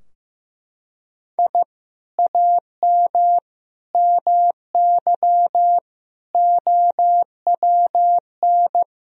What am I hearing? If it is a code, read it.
Morse code, heard as a keyed tone, about 15 wpm, 700 Hz. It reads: I AM MY OWN
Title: Sentences from Top 1000 Words - Review Part 4 - 15wpm